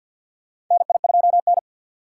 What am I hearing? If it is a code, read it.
Morse code, heard as a keyed tone, about 52 wpm, 700 Hz. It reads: NI2N
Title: USA Call Signs - 2x Character Spacing - 50wpm